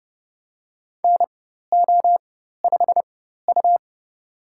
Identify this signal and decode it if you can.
Morse code, heard as a keyed tone, about 32 wpm, spaced as for 14 wpm, 700 Hz. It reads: NO5U
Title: USA Call Signs - 4 Characters - 4x Character Spacing - 30wpm